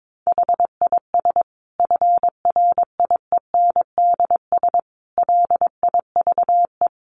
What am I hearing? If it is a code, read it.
Morse code, heard as a keyed tone, about 22 wpm, 700 Hz. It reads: HIS FRIENDS LIVE
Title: Additional Easy Phrases 22wpm